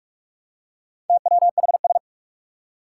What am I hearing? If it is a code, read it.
Morse code, heard as a keyed tone, about 45 wpm, 700 Hz. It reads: TWHS
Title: Four Challenging Letter-Number Combinations 45wpm